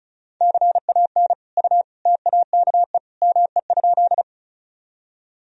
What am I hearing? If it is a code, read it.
Morse code, heard as a keyed tone, about 34 wpm, 700 Hz. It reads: CAN U TAKE ME?
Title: Sentences from Top 200 Words and Abbreviations 35wpm